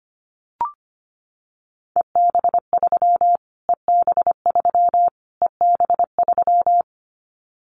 Morse code, heard as a keyed tone, about 25 wpm, 700 Hz. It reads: EB3 EB3 EB3
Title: Callsign Prefixes for Top 25 DXCC - Repeated 3x - 25wpm